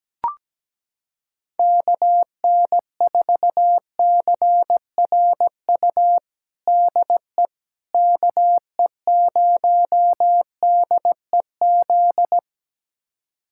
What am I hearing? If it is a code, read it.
Morse code, heard as a keyed tone, about 17 wpm, 700 Hz. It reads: KN4CRU DE KE0DEZ
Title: USA Callsign DE Callsign - 6 Characters - 17wpm